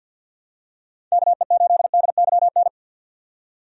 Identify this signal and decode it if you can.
Morse code, heard as a keyed tone, about 51 wpm, 700 Hz. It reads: KE8DYN
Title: USA Call Signs - 6 Characters - 50wpm